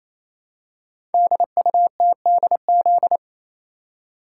Morse code, heard as a keyed tone, about 29 wpm, 700 Hz. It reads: DUTDZ
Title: Code Groups - 5 Letters - 28wpm